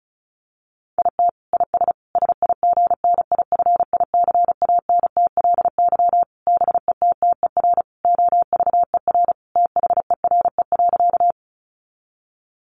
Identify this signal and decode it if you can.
Morse code, heard as a keyed tone, about 34 wpm, 700 Hz. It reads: IT IS SIGNIFICANTLY BETTER OVER THERE.
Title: Sentences from Top 1500 Words 35wpm